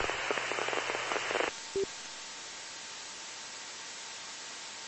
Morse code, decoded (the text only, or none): none